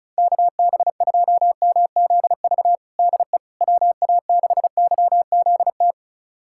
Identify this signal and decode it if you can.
Morse code, heard as a keyed tone, about 36 wpm, 690 Hz. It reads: KB2MZV DE WA6YZT
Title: USA Callsign DE Callsign - 6 Characters - 35wpm